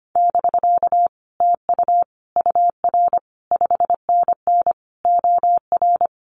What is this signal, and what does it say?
Morse code, heard as a keyed tone, about 25 wpm, 700 Hz. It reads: <BK> TU UR 5NN OR
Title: Common QSO Phrases Encore 25wpm